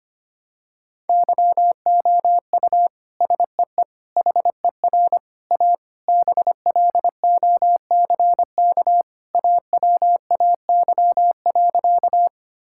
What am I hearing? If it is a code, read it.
Morse code, heard as a keyed tone, about 25 wpm, 700 Hz. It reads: YOU SEE HER A BLOCK AWAY.
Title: Sentences from Top 1600 Words 25wpm